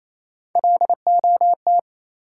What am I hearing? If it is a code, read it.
Morse code, heard as a keyed tone, about 28 wpm, 700 Hz. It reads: LOT
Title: Sentences from Top 500 Words - New Word Drill - 28wpm